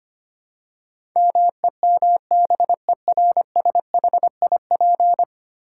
Morse code, heard as a keyed tone, about 25 wpm, 690 Hz. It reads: MEMBERSHIP
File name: 10 Letter Words - First 3 Letters Given - 25wpm